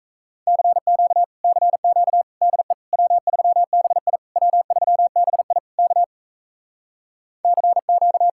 Morse code, heard as a keyed tone, about 40 wpm, 700 Hz. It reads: CQ CQ DE W3BI W3BI K CQ